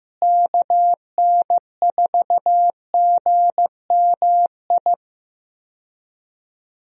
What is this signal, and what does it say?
Morse code, heard as a keyed tone, about 15 wpm, 700 Hz. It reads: KN4GMI
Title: USA Call Signs - 6 Characters - Repeated 3x - 15wpm